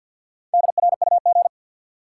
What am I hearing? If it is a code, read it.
Morse code, heard as a keyed tone, about 53 wpm, 700 Hz. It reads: DRUG
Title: Sets of 5 Words - New Word Drill - 50wpm